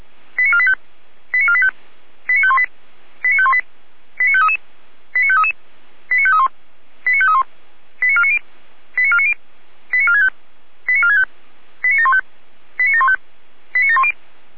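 Five-tone selective call (ZVEI), heard as quick runs of five tones, tones of about 70 ms. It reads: 89485 89485 89518 89518 89533 89533 89531 89531 89490 89490 89465 89465 89715 89715 89710
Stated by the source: ZVEI five-tone selective call received by SDR.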